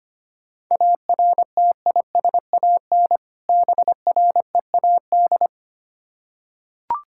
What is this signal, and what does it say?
Morse code, heard as a keyed tone, about 25 wpm, 700 Hz. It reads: ARTISAN BREAD
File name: practice / 25wpm / Difficult Words with Hints - 25wpm